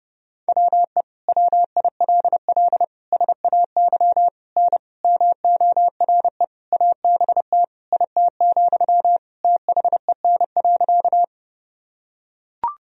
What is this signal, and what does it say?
Morse code, heard as a keyed tone, about 30 wpm, 700 Hz. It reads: WE WILL SAY N MORE ABT IT, THEN.
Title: Sentences from Top 200 Words and Abbreviations 30wpm